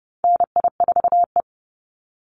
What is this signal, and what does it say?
Morse code, heard as a keyed tone, about 30 wpm, 700 Hz. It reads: NI4E
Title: Call Signs Repeated 3x - 30wpm